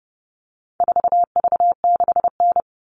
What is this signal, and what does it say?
Morse code, heard as a keyed tone, about 29 wpm, 700 Hz. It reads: 4V6N
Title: All Cut Numbers - 4-Digit 30wpm